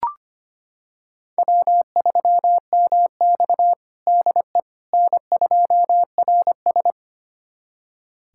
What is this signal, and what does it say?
Morse code, heard as a keyed tone, about 25 wpm, 700 Hz. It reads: W3MX DE N2RS
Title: USA Callsign DE Callsign - 4 Characters - 25wpm